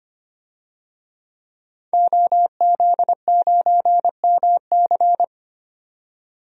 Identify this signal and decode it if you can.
Morse code, heard as a keyed tone, about 25 wpm, 700 Hz. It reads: OZ9MC
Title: DXCC-100 Call Signs - 5 Characters - 25wpm